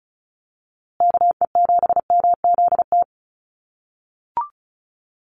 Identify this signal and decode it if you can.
Morse code, heard as a keyed tone, about 34 wpm, 700 Hz. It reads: KE7MZT
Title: USA Call Signs Repeated 2x 35wpm